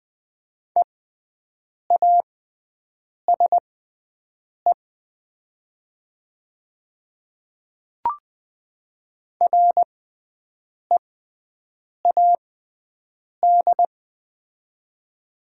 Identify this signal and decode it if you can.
Morse code, heard as a keyed tone, about 20 wpm, spaced as for 7 wpm, 700 Hz. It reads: EASE READ
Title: 6x Character Spacing - Top 500 Words - 20wpm